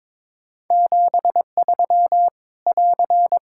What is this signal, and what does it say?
Morse code, heard as a keyed tone, about 22 wpm, 700 Hz. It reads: 73 <AR>
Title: Practice QSO - Exchange 4 - Ending - 22wpm